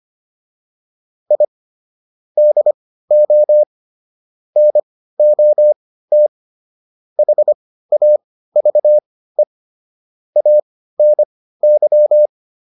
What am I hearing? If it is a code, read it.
Morse code, heard as a keyed tone, about 25 wpm, spaced as for 15 wpm, 600 Hz. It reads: I DO NOT HAVE ANY
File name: WD405 15